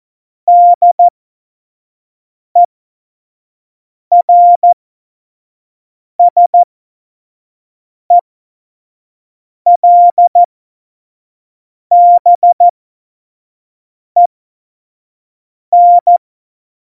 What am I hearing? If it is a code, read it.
Morse code, heard as a keyed tone, about 14 wpm, spaced as for 5 wpm, 700 Hz. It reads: DERSELBEN
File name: Morsekurs / Witze / Witz 04. Juni 2019, 25 F70